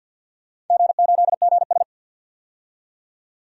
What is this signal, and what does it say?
Morse code, heard as a keyed tone, about 52 wpm, 700 Hz. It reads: G8GS
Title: DXCC-100 Call Signs - 4 Characters - 50wpm